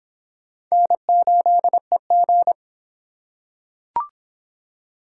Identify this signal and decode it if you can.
Morse code, heard as a keyed tone, about 25 wpm, 700 Hz. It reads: N8EG